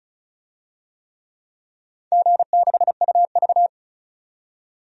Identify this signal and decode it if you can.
Morse code, heard as a keyed tone, about 36 wpm, 700 Hz. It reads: G6UV
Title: DXCC-100 Call Signs - 4 Characters - 35wpm